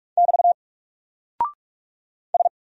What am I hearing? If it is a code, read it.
Morse code, heard as a keyed tone, about 46 wpm, 700 Hz. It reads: <BT> S
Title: Single Character (Letter:Number:Punctuation:Prosign) 45wpm